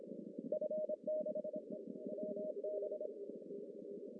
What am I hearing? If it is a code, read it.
Morse code, heard as a keyed tone, about 25 wpm, spaced as for 19 wpm, 610 Hz. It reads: F6WB
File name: CW / cqww-cw-k5tr